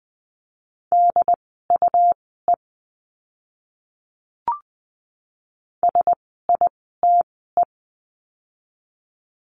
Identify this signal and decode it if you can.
Morse code, heard as a keyed tone, about 20 wpm, spaced as for 14 wpm, 700 Hz. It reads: DUE SITE
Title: 101 Lesson 17f - Letter u - Words 20wpm